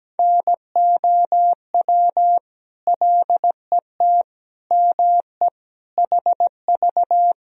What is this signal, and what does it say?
Morse code, heard as a keyed tone, about 17 wpm, 700 Hz. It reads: NOW LET ME HV
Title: Sentences from Top 300 Words and Abbreviations 17wpm